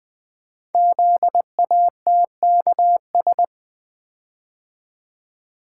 Morse code, heard as a keyed tone, about 20 wpm, 700 Hz. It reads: ZATKS
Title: Code Groups - 5 Letters - 20wpm